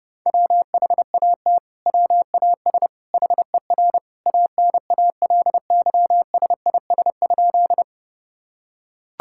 Morse code, heard as a keyed tone, about 32 wpm, 700 Hz. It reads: WHAT WAS HER ANALYSIS?
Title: Sentences from Top 1000 Words - Review Part 5 - 30wpm